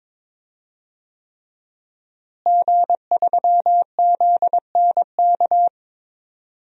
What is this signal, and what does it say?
Morse code, heard as a keyed tone, about 22 wpm, 700 Hz. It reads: G3ZNK
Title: DXCC-100 Call Signs - 5 Characters - Repeated 3x - 22wpm